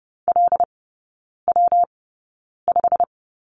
Morse code, heard as a keyed tone, about 30 wpm, spaced as for 9 wpm, 700 Hz. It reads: LW5
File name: Lesson 19e - Letter w - Three Characters 30wpm